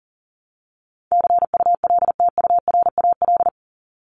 Morse code, heard as a keyed tone, about 40 wpm, 700 Hz. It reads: CULTURAL
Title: Top 1251-1500 Words 40wpm